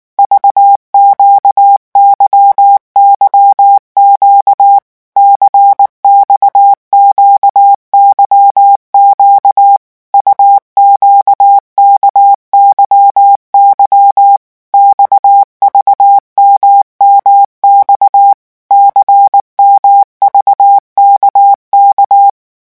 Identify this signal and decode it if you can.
Morse code, heard as a keyed tone, about 19 wpm, 800 Hz. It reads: VQYYQ CXQYQ UQKYY XVMMX CMVKK
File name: morse mp3 / revision2-vitesse 18 mots